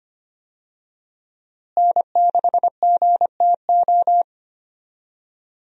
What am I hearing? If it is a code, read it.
Morse code, heard as a keyed tone, about 25 wpm, 700 Hz. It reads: N6GTO
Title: USA Call Signs - 5 Characters - Repeated 3x - 25wpm